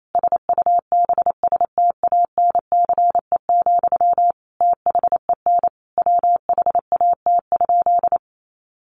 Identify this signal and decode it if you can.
Morse code, heard as a keyed tone, about 28 wpm, 700 Hz. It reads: SUBSTANCE, THEN WHAT?
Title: Sentences from Top 1500 Words 28wpm